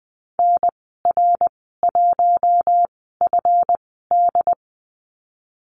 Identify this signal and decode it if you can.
Morse code, heard as a keyed tone, about 20 wpm, spaced as for 15 wpm, 700 Hz. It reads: NR1FD
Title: 147 Lesson 24g - Letter y - Call Signs 20wpm